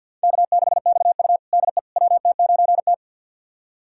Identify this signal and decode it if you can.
Morse code, heard as a keyed tone, about 46 wpm, 700 Hz. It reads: K6XU DE WT9T
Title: USA Callsign DE Callsign - 4 Characters - 50wpm